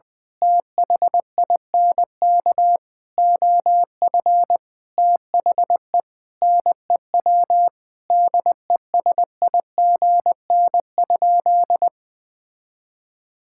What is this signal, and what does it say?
Morse code, heard as a keyed tone, about 21 wpm, 700 Hz. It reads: THINK OF THE NEW DESIGN?
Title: Sentences with 5-Word Phrases - Spoken Beginning - 20wpm